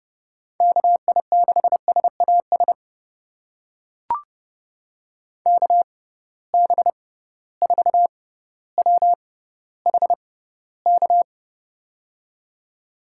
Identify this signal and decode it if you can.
Morse code, heard as a keyed tone, about 31 wpm, spaced as for 16 wpm, 700 Hz. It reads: KI6SAS KB4WHK
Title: USA Call Signs - 6 Characters - 6x Character Spacing - 30wpm